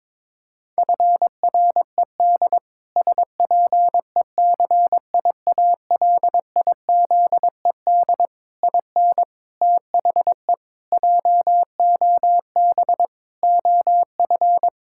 Morse code, heard as a keyed tone, about 22 wpm, 700 Hz. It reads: FRED SPECIALIZED IN THE JOB OF
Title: Pangram Challenge v2 - 22wpm